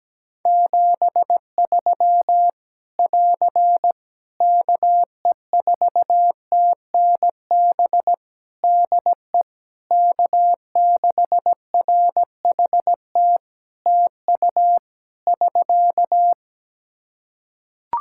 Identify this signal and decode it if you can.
Morse code, heard as a keyed tone, about 17 wpm, 700 Hz. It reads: 73 <AR> KE4TNB DE K6RHT TU <SK>
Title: Practice QSO - Exchange 4 - Ending - 17wpm